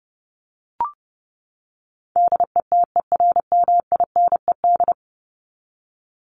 Morse code, heard as a keyed tone, about 30 wpm, 700 Hz. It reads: DETERMINED
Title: ED Suffix - Top 100 Words with Base Words 30wpm